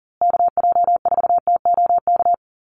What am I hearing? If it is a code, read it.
Morse code, heard as a keyed tone, about 40 wpm, 700 Hz. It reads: KJ4TOK
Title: USA Call Signs Repeated 3x 40wpm